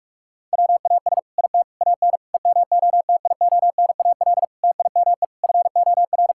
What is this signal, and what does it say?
Morse code, heard as a keyed tone, about 45 wpm, 700 Hz. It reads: WAS IT AN EMOTIONAL TIME FOR
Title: Sentences from Top 1000 Words - Review Part 5 - 45wpm